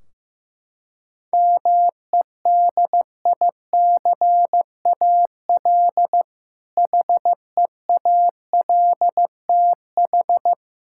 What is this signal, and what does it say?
Morse code, heard as a keyed tone, about 15 wpm, 700 Hz. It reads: MEDICAL HEALTH
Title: K1USN SST QSO Practice 15wpm